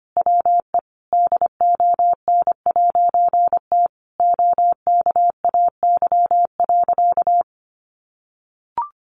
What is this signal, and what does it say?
Morse code, heard as a keyed tone, about 25 wpm, 700 Hz. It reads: WE DON'T OKAY.
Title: A Sound Of Thunder 25wpm